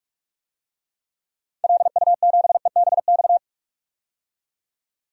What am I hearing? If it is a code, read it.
Morse code, heard as a keyed tone, about 43 wpm, 700 Hz. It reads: LU7EBX
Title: DXCC-100 Call Signs - 6 Characters - 45wpm